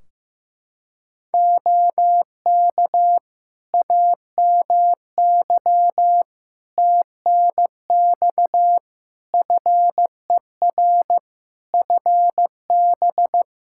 Morse code, heard as a keyed tone, about 15 wpm, 700 Hz. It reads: OK AMY TNX FER FB
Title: Practice QSO - Exchange 4 - Ending - 15wpm